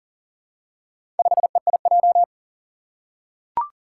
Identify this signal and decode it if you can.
Morse code, heard as a keyed tone, about 40 wpm, 700 Hz. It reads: 5EIJ
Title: Four Challenging Letter-Number Combinations 40wpm